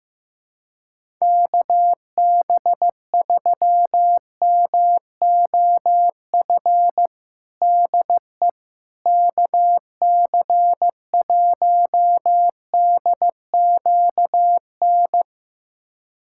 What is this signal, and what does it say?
Morse code, heard as a keyed tone, about 15 wpm, 700 Hz. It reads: KB3MOF DE KC1DQN
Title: USA Callsign DE Callsign 15wpm